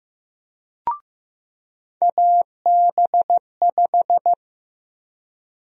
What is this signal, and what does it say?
Morse code, heard as a keyed tone, about 15 wpm, 700 Hz. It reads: AB5
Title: USA Call Sign Prefixes - 15wpm